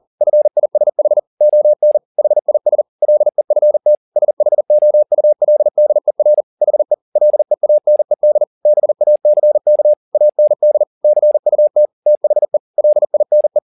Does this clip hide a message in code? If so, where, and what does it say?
Yes: Morse code from 0.2 to 13.6 s, heard as a keyed tone, about 40 wpm, 600 Hz. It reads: FISH ON HIS LEFT SHOULDER HE LEANED BACK AND CUT THE LINE